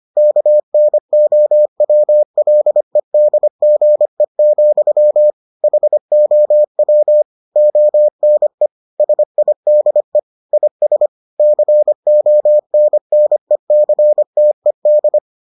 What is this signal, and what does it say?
Morse code, heard as a keyed tone, about 25 wpm, 600 Hz. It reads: KNOWLEDGE, HOW ONE SIDE IS CONNECTED